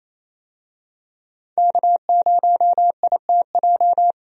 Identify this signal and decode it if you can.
Morse code, heard as a keyed tone, about 27 wpm, 700 Hz. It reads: K0ITJ
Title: USA Call Signs - 5 Characters - Repeated 3x - 28wpm